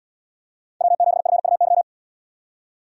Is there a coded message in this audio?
Yes: Morse code from 0.8 to 1.8 s, heard as a keyed tone, about 72 wpm, 700 Hz. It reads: W7FUY